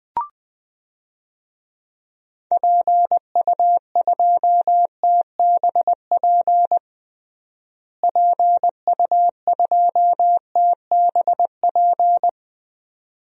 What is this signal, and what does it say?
Morse code, heard as a keyed tone, about 20 wpm, 700 Hz. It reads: PU2TBP PU2TBP
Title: DXCC-100 Call Signs - 6 Characters - Repeated 2x - 20wpm